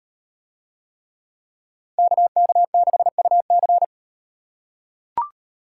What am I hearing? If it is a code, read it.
Morse code, heard as a keyed tone, about 38 wpm, 700 Hz. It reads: KK6UC